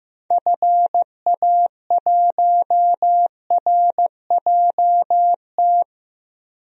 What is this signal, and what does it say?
Morse code, heard as a keyed tone, about 16 wpm, 700 Hz. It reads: FA1RJT
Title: DXCC-100 Call Signs - 6 Characters - 15wpm